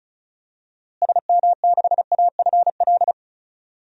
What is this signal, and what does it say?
Morse code, heard as a keyed tone, about 34 wpm, 700 Hz. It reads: SM6AFL
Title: DXCC-100 Call Signs - 6 Characters - Repeated 3x - 35wpm